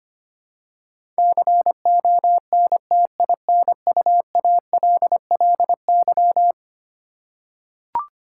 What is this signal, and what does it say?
Morse code, heard as a keyed tone, about 25 wpm, 700 Hz. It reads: CONTINUALLY
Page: morse-code-ninja-practice-25wpm